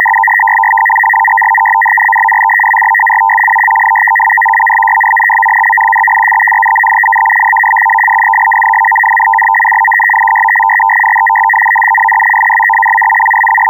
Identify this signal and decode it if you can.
Two-tone teleprinter signal, about 850 Hz shift.